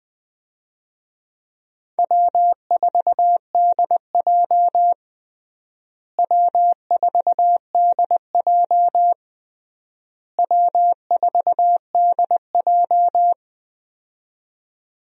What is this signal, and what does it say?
Morse code, heard as a keyed tone, about 20 wpm, 690 Hz. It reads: W4DJ W4DJ W4DJ
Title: USA Call Signs - 4 Characters - Repeated 3x - 20wpm